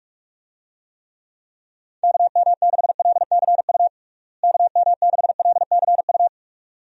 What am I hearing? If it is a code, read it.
Morse code, heard as a keyed tone, about 46 wpm, 700 Hz. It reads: KM6LCU KM6LCU
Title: USA Call Signs - 6 Characters - Repeated 3x - 45wpm